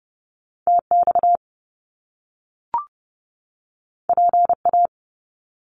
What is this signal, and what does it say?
Morse code, heard as a keyed tone, about 30 wpm, 700 Hz. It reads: TX PA
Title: US States 30wpm